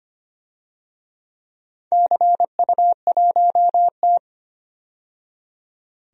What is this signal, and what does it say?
Morse code, heard as a keyed tone, about 25 wpm, 700 Hz. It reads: CU1T
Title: DXCC-100 Call Signs - 4 Characters - 25wpm